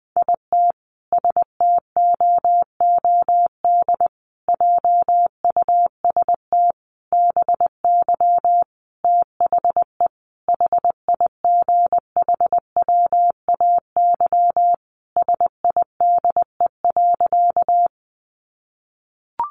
Morse code, heard as a keyed tone, about 20 wpm, 700 Hz. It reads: IT STOOD JUST BY THE HIGHWAY SIDE.
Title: Sentences from Top 1400 Words 20wpm